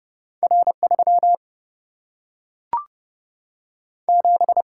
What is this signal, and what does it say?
Morse code, heard as a keyed tone, about 30 wpm, 700 Hz. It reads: R 3 7
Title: Single-Letter Single-Number 30wpm